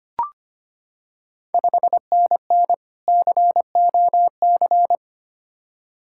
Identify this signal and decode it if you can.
Morse code, heard as a keyed tone, about 25 wpm, 700 Hz. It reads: 5NN COC
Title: Ontario QSO Party Exchange 25wpm